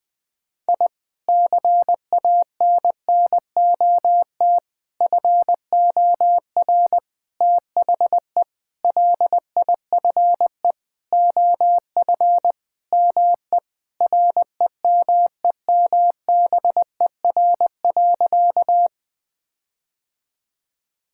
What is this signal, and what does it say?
Morse code, heard as a keyed tone, about 20 wpm, 700 Hz. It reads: I CANNOT FOR THE LIFE OF ME REMEMBER.
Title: Sentences with 8-Word Phrases - 20wpm